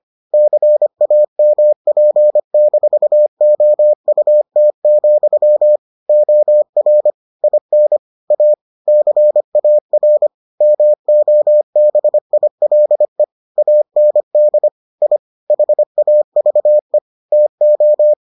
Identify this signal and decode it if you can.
Morse code, heard as a keyed tone, about 25 wpm, 600 Hz. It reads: CAMP-OUT, OR IN A CAR MOBILE AND I HAVE TO